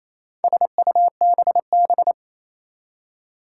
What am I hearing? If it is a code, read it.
Morse code, heard as a keyed tone, about 28 wpm, 700 Hz. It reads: SUBB
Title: Four Challenging Letter-Number Combinations 28wpm